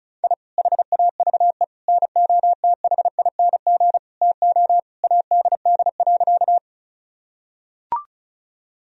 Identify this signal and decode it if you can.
Morse code, heard as a keyed tone, about 34 wpm, 700 Hz. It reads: I HAVE NOTHING TO ADD.
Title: Sentences from Top 1000 Words - Review Part 1 - 35wpm